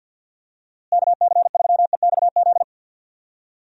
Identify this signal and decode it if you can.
Morse code, heard as a keyed tone, about 52 wpm, 700 Hz. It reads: KC3EXZ
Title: USA Call Signs - 6 Characters - Repeated 3x - 50wpm